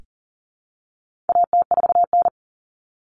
Morse code, heard as a keyed tone, about 40 wpm, 700 Hz. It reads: AT4N
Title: Common Cut Numbers - 4-Digit 40wpm